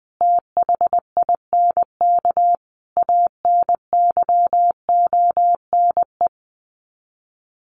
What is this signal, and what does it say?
Morse code, heard as a keyed tone, about 19 wpm, 700 Hz. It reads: THINK ANYONE
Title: Sets of 5 Words Encore 20wpm